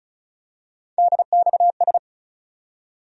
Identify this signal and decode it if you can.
Morse code, heard as a keyed tone, about 36 wpm, 700 Hz. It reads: DXS
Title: USA Call Sign Postfixes - 3 Letters 35wpm